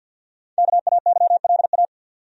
Morse code, heard as a keyed tone, about 53 wpm, 700 Hz. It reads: KAYLA
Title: USA Common Names - Max 5 Letters - 50wpm